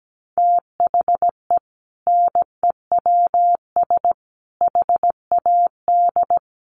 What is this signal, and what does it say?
Morse code, heard as a keyed tone, about 17 wpm, 700 Hz. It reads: THE NEWS HAD
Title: Sentences from Top 1200 Words 17wpm